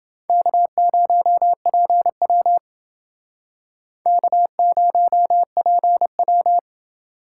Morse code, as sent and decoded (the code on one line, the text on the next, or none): -.- ----- .--. .-- / -.- ----- .--. .--
K0PW K0PW